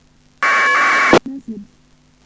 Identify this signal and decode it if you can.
Packet radio (AX.25, as in APRS), with speech.
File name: uploader / iss aprs